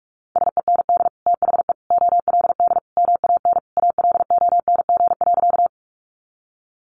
Morse code, heard as a keyed tone, about 44 wpm, 700 Hz. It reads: SEND THE OLD MAN ALONG.
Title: Sentences from Top 1500 Words 45wpm